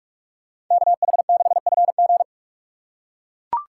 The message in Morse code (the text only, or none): KH6FG